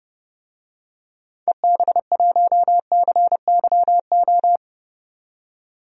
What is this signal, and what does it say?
Morse code, heard as a keyed tone, about 30 wpm, 700 Hz. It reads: EB1CYO
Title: DXCC-100 Call Signs - 6 Characters - 30wpm